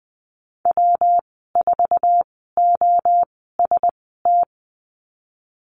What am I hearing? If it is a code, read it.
Morse code, heard as a keyed tone, about 20 wpm, spaced as for 15 wpm, 700 Hz. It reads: W4OST
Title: 200 Lesson 32g - Letter k - Call Signs 20wpm